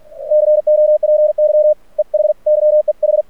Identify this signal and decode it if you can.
Morse code, heard as a keyed tone, about 46 wpm, 610 Hz. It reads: COPY EH YEH
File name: archived / CRSnet